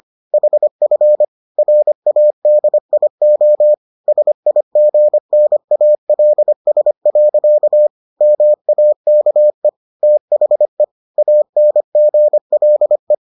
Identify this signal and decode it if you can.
Morse code, heard as a keyed tone, about 25 wpm, 600 Hz. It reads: HF RADIO SIGNALS. MAKE THE ANGLE